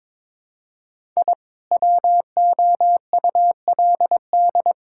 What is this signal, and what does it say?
Morse code, heard as a keyed tone, about 22 wpm, 700 Hz. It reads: I WOULD
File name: practice / 22wpm / Build A Phrase - 5-Word Phrases - 22wpm